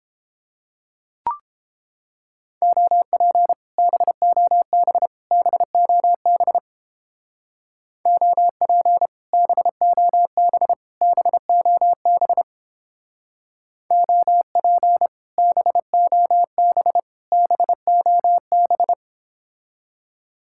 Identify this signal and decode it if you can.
Morse code, heard as a keyed tone, about 29 wpm, 700 Hz. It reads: OP BOB BOB OP BOB BOB OP BOB BOB